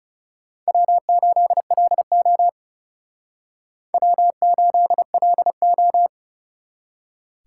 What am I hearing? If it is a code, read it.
Morse code, heard as a keyed tone, about 34 wpm, 700 Hz. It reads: W8LO W8LO
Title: Speed Racing - USA 1x2 Call Signs 30wpm